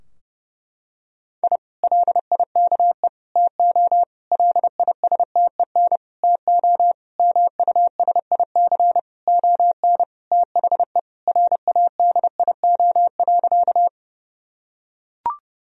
Morse code, heard as a keyed tone, about 30 wpm, 700 Hz. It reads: I LIKE TO LISTEN TO MUSIC ON THE RADIO.
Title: Sentences from Top 1000 Words - Review Part 3 - 30wpm